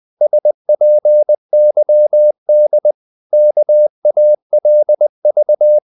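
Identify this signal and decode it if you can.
Morse code, heard as a keyed tone, about 20 wpm, 600 Hz. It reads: SPYD KALV
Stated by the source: Morse code practice of short Danish words at 20 wpm, from CW med Gnister.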